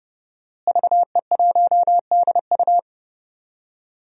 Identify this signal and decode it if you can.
Morse code, heard as a keyed tone, about 30 wpm, 700 Hz. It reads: VE1DU
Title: DXCC-100 Call Signs - 5 Characters - Repeated 2x - 30wpm